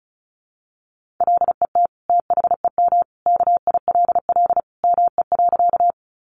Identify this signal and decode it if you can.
Morse code, heard as a keyed tone, about 34 wpm, 700 Hz. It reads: LET THEM KILL ME.
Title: Sentences from Top 1400 Words 35wpm